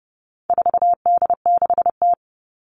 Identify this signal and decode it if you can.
Morse code, heard as a keyed tone, about 30 wpm, 700 Hz. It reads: 4D6T
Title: All Cut Numbers - 4-Digit 30wpm